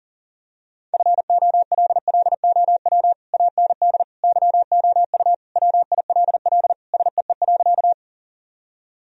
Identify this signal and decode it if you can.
Morse code, heard as a keyed tone, about 40 wpm, 700 Hz. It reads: FOLLOW AND YOU WILL SEE.
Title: Sentences from Top 1000 Words - Review Part 1 - 40wpm